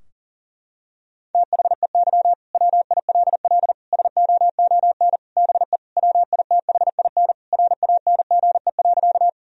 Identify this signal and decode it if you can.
Morse code, heard as a keyed tone, about 37 wpm, 700 Hz. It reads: THEY WILL SOON BE WITHIN RANGE.
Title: Sentences from Top 1000 Words - Review Part 4 - 40wpm